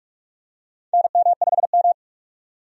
Morse code, heard as a keyed tone, about 46 wpm, 700 Hz. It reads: NM5M
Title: USA Call Signs - 4 Characters - 45wpm